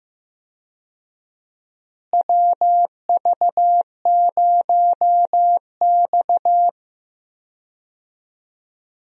Morse code, heard as a keyed tone, about 15 wpm, 700 Hz. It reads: WV0X
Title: USA Call Signs - 4 Characters - Repeated 2x - 15wpm